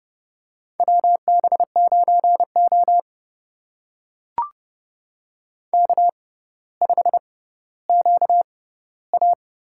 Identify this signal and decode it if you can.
Morse code, heard as a keyed tone, about 29 wpm, spaced as for 15 wpm, 700 Hz. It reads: WB9O K5QA
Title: USA Call Signs - 4 Characters - 6x Character Spacing - 30wpm